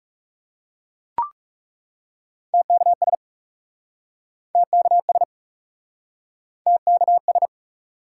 Morse code, heard as a keyed tone, about 40 wpm, 700 Hz. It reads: TKS TKS TKS